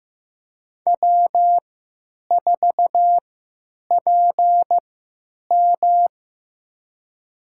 Morse code, heard as a keyed tone, about 15 wpm, spaced as for 9 wpm, 700 Hz. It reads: W4PM
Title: USA Call Signs - 4 Characters - 3x Character Spacing - 15wpm